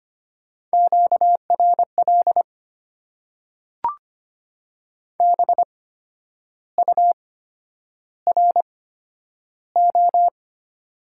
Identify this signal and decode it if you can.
Morse code, heard as a keyed tone, about 25 wpm, spaced as for 10 wpm, 700 Hz. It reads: QRL BURO
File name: Top 90 QSO Elements - 8x Character Spacing - 25wpm